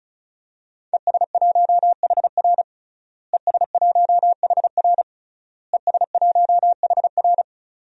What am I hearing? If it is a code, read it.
Morse code, heard as a keyed tone, about 34 wpm, 700 Hz. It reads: ES1HR ES1HR ES1HR
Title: DXCC-100 Call Signs - 5 Characters - Repeated 3x - 35wpm